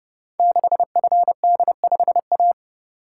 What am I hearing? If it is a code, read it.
Morse code, heard as a keyed tone, about 30 wpm, 700 Hz. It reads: 6FD5A
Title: Code Groups - 5 Letter:Numbers - 30wpm